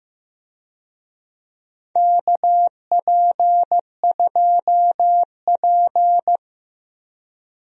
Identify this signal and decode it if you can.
Morse code, heard as a keyed tone, about 16 wpm, 700 Hz. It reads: KP2P
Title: DXCC-100 Call Signs - 4 Characters - Repeated 2x - 15wpm